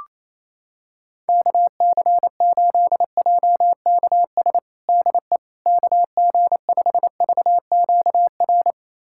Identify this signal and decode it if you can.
Morse code, heard as a keyed tone, about 28 wpm, 700 Hz. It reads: KC8JKS DE KG5VQR
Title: USA Callsign DE Callsign - 6 Characters - 28wpm